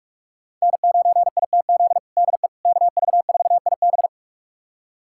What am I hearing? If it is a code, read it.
Morse code, heard as a keyed tone, about 46 wpm, 700 Hz. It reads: N9ITZ DE KV4IB
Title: USA Callsign DE Callsign - 5 Characters - 45wpm